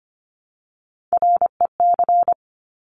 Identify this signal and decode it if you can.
Morse code, heard as a keyed tone, about 25 wpm, 700 Hz. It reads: REC
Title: Top Three Letter Combinations in Words 25wpm